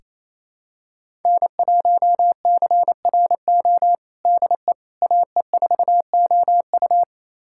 Morse code, heard as a keyed tone, about 28 wpm, 700 Hz. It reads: N1CRO DE AE4OU
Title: USA Callsign DE Callsign - 5 Characters - 28wpm